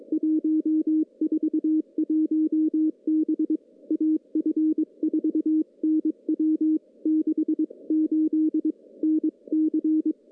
Morse code, heard as a keyed tone, about 22 wpm, 310 Hz. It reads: 141 B AF4NW 68 NC
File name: contests / CW-SS-k5tr- -0400